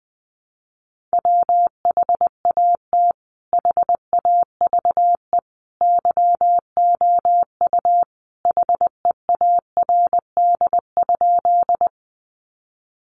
Morse code, heard as a keyed tone, about 20 wpm, 700 Hz. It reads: WHAT HAVE YOU HEARD?
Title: Sentences from Top 400 Words 20wpm